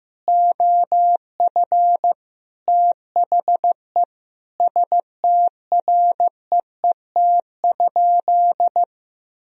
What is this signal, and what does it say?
Morse code, heard as a keyed tone, about 15 wpm, 700 Hz. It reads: OF THE STREET?
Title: Sentences from Top 1000 Words - Review Part 3 - 15wpm